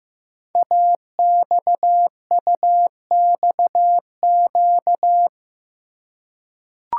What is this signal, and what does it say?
Morse code, heard as a keyed tone, about 15 wpm, 700 Hz. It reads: AXUXQ
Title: Code Groups - 5 Letters - 15wpm